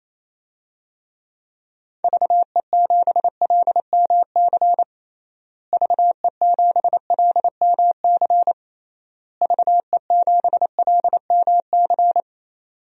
Morse code, heard as a keyed tone, about 27 wpm, 700 Hz. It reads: VE7LMC VE7LMC VE7LMC
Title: DXCC-100 Call Signs - 6 Characters - Repeated 3x - 28wpm